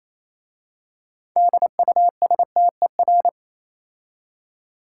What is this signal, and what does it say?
Morse code, heard as a keyed tone, about 27 wpm, 690 Hz. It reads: DUSTER
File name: Easy Words - Speed-Racing with Hints - 22wpm